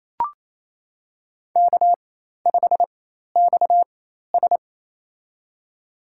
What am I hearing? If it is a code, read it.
Morse code, heard as a keyed tone, about 28 wpm, spaced as for 13 wpm, 700 Hz. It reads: K5XS
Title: USA Call Signs - 4x Character Spacing - 28wpm